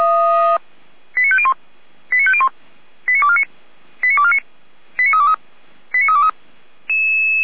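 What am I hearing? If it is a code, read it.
Five-tone selective call (ZVEI), heard as quick runs of five tones, tones of about 70 ms. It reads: 89501 89501 89249 89249 89323 89323
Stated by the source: ZVEI five-tone selective call received by SDR, fire brigade Orlamünde (feuerwehr-orlamuende).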